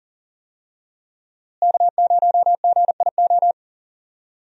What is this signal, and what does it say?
Morse code, heard as a keyed tone, about 40 wpm, 700 Hz. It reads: K0GIO
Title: USA Call Signs - 5 Characters - Repeated 3x - 40wpm